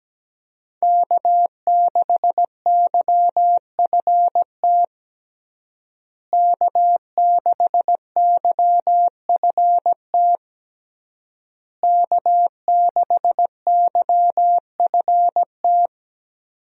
Morse code, heard as a keyed tone, about 17 wpm, 700 Hz. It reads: K6YFT K6YFT K6YFT
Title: USA Call Signs - 5 Characters - Repeated 3x - 17wpm